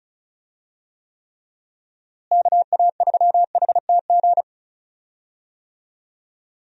Morse code, heard as a keyed tone, about 34 wpm, 700 Hz. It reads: KA3HTG
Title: USA Call Signs - 6 Characters - Repeated 3x - 35wpm